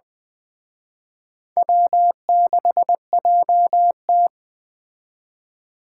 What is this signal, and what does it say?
Morse code, heard as a keyed tone, about 21 wpm, 700 Hz. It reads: W6JT